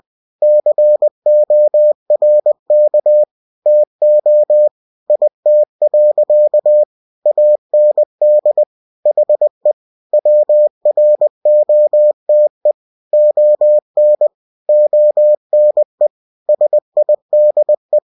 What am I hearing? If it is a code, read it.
Morse code, heard as a keyed tone, about 20 wpm, 600 Hz. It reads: CORK TO IT. AND HE WROTE ON ONE SIDE
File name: cw-books / winnie-the-pooh / winnie-20wpm-page-0092